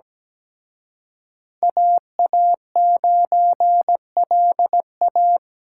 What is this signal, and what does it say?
Morse code, heard as a keyed tone, about 20 wpm, 700 Hz. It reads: AA9LA
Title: Speed Racing - USA 2x2 Call Signs 17wpm